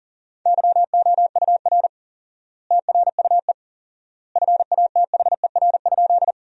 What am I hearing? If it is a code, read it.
Morse code, heard as a keyed tone, about 40 wpm, 700 Hz. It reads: YOUR TRUE FATHER?
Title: 4x Word-Spacing - Sentences from Top 400 Words - 40wpm